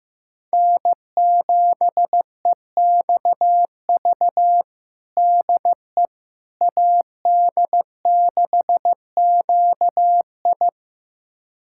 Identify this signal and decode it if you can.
Morse code, heard as a keyed tone, about 15 wpm, 700 Hz. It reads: N7EXV DE AD6QI
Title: USA Callsign DE Callsign - 5 Characters - 15wpm